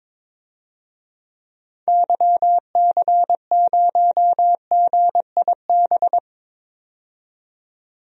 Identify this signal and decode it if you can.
Morse code, heard as a keyed tone, about 22 wpm, 700 Hz. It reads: YC0GIB
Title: DXCC-100 Call Signs - 6 Characters - 22wpm